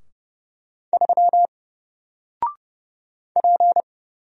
Morse code, heard as a keyed tone, about 30 wpm, 700 Hz. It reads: 3 P
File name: Single-Letter-Number-30wpm